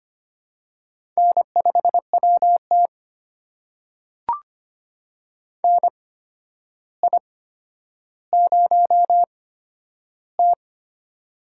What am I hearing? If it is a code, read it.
Morse code, heard as a keyed tone, about 25 wpm, spaced as for 11 wpm, 700 Hz. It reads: N5WT NI0T
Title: USA Call Signs - 4 Characters - 8x Character Spacing - 25wpm